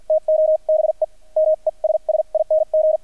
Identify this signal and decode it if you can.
Morse code, heard as a keyed tone, about 48 wpm, spaced as for 40 wpm, 630 Hz, with radio noise. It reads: TO BE MESSING
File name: archived / CRSnet